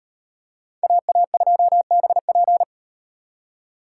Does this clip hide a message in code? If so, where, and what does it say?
Yes: Morse code from 0.8 to 2.6 s, heard as a keyed tone, about 36 wpm, 700 Hz. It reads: AA2BP